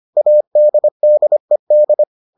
Morse code, heard as a keyed tone, about 25 wpm, 600 Hz. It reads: ADDED